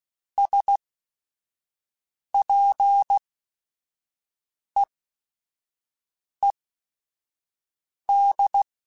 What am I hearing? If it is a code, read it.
Morse code, heard as a keyed tone, about 16 wpm, spaced as for 5 wpm, 800 Hz. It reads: SPEED
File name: Weather Sets / Weather Sets-mp3s / Weather 5 5 @5